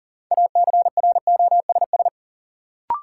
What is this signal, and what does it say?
Morse code, heard as a keyed tone, about 40 wpm, 700 Hz. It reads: ACROSS
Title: Sentences from Top 700 Words - New Word Drill - 40wpm